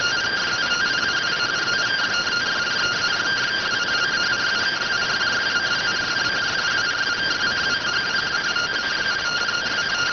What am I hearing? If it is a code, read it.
Two-tone teleprinter signal, about 170 Hz shift.